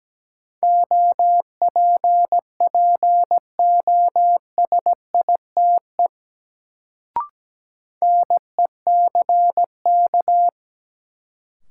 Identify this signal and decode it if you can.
Morse code, heard as a keyed tone, about 17 wpm, 700 Hz. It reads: OPPOSITE NECK
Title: Sets of 5 Words Encore - New Word Drill - 17wpm